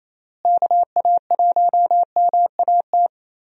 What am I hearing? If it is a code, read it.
Morse code, heard as a keyed tone, about 28 wpm, 700 Hz. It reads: KA1MAT
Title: USA Call Signs - 6 Characters - Repeated 3x - 28wpm